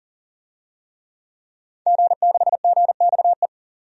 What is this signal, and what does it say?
Morse code, heard as a keyed tone, about 40 wpm, 700 Hz. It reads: G6GXE
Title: DXCC-100 Call Signs - 5 Characters - 40wpm